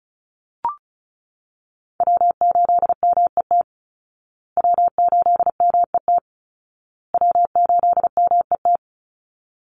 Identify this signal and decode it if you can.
Morse code, heard as a keyed tone, about 34 wpm, 700 Hz. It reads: W8MET W8MET W8MET
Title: Call Signs Repeated 2x - 35wpm